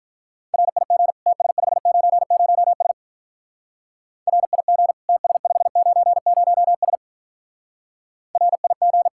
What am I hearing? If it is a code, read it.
Morse code, heard as a keyed tone, about 44 wpm, 700 Hz. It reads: RIG TS590S RIG TS590S RIG